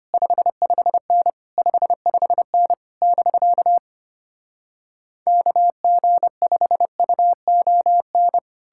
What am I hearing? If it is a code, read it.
Morse code, heard as a keyed tone, about 33 wpm, 700 Hz. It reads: 55N 55N <BK> KG5UON